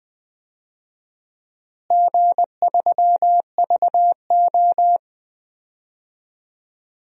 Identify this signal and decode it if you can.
Morse code, heard as a keyed tone, about 20 wpm, 700 Hz. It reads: G3VO
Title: DXCC-100 Call Signs - 4 Characters - Repeated 3x - 20wpm